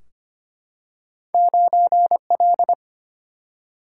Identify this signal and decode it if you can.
Morse code, heard as a keyed tone, about 25 wpm, 700 Hz. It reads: 9L
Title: Single-Letter Single-Number 25wpm